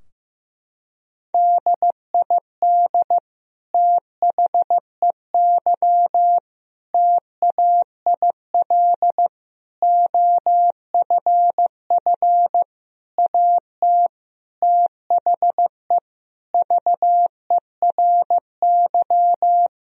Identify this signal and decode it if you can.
Morse code, heard as a keyed tone, about 15 wpm, 700 Hz. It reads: DID THEY TAIL OFF AT THE VERY
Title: Sentences from Top 1000 Words - Review Part 3 - 15wpm